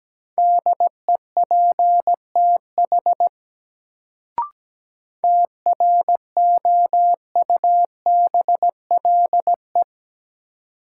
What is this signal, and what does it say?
Morse code, heard as a keyed tone, about 17 wpm, 700 Hz. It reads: DEPTH TROUBLE
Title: Sentences from Top 1200 Words - New Word Drill - 17wpm